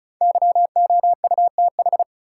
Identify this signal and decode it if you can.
Morse code, heard as a keyed tone, about 35 wpm, 700 Hz. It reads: YOUTH
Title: Five Letter Words - 35wpm